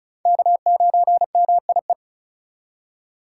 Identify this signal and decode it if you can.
Morse code, heard as a keyed tone, about 34 wpm, 700 Hz. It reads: K9MIE
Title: USA Call Signs - 5 Characters - 4x Character Spacing - 35wpm